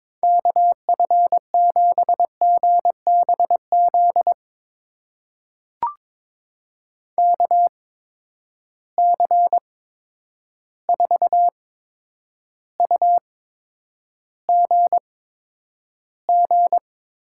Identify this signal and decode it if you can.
Morse code, heard as a keyed tone, about 22 wpm, spaced as for 9 wpm, 700 Hz. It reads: KF7GBZ KC4UGG
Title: USA Call Signs - 6 Characters - 8x Character Spacing - 22wpm